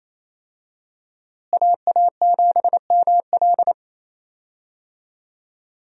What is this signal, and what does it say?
Morse code, heard as a keyed tone, about 28 wpm, 700 Hz. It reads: AA7ML
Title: USA Call Signs - 5 Characters - Repeated 3x - 28wpm